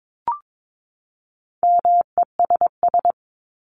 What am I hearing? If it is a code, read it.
Morse code, heard as a keyed tone, about 22 wpm, 700 Hz. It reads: MESS
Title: Top 1751-2000 Words 22wpm